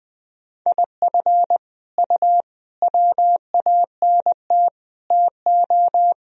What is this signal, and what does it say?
Morse code, heard as a keyed tone, about 20 wpm, 700 Hz. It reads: IF U WANT TO
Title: Sentences from Top 200 Words and Abbreviations 20wpm